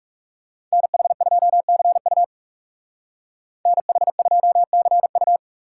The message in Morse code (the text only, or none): NH2CU NH2CU